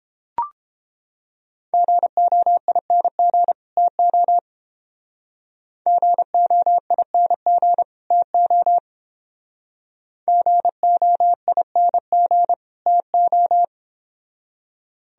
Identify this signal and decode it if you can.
Morse code, heard as a keyed tone, about 31 wpm, 700 Hz. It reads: GOING TO GOING TO GOING TO